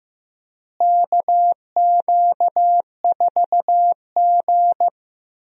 Morse code, heard as a keyed tone, about 15 wpm, 700 Hz. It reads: KQ4G
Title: USA Call Signs - 4 Characters - Repeated 2x - 15wpm